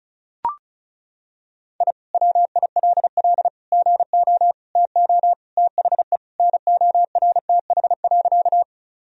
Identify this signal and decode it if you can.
Morse code, heard as a keyed tone, about 34 wpm, 700 Hz. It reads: I WILL GO TO THE NORTH.
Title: Sentences from Top 1000 Words - Review Part 1 - 35wpm